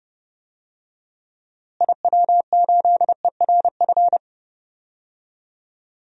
Morse code, heard as a keyed tone, about 30 wpm, 700 Hz. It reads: IW8ERF
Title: DXCC-100 Call Signs - 6 Characters - Repeated 3x - 30wpm